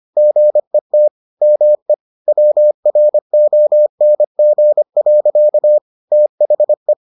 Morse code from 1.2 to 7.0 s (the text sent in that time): ME WRONG. THE